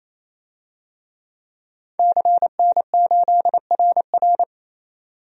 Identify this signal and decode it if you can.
Morse code, heard as a keyed tone, about 28 wpm, 700 Hz. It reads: CN8RR
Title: DXCC-100 Call Signs - 5 Characters - 28wpm